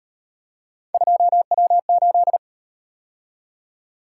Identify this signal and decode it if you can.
Morse code, heard as a keyed tone, about 38 wpm, 700 Hz. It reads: 2W8